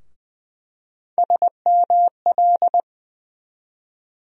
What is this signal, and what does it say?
Morse code, heard as a keyed tone, about 20 wpm, 700 Hz. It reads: SML
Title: USA Call Sign Postfixes - 3 Letters 20wpm